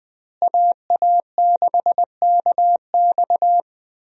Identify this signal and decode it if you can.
Morse code, heard as a keyed tone, about 20 wpm, 700 Hz. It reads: AA6KX
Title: USA Call Signs - 5 Characters - Repeated 2x - 20wpm